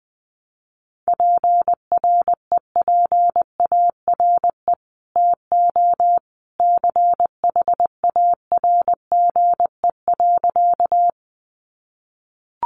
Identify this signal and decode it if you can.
Morse code, heard as a keyed tone, about 20 wpm, 700 Hz. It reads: PREPARE TO CHARGE.
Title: Sentences from Top 900 Words 20wpm